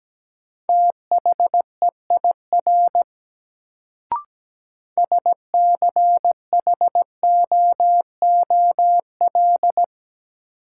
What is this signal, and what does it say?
Morse code, heard as a keyed tone, about 17 wpm, 700 Hz. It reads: THEIR SCHOOL
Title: Sets of 2 Words - New Word Drill - 17wpm